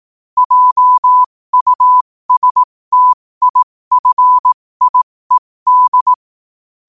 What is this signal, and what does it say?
Morse code, heard as a keyed tone, about 18 wpm, spaced as for 15 wpm, 1000 Hz. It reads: JUSTIFIED
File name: qotdfiles / 15wpm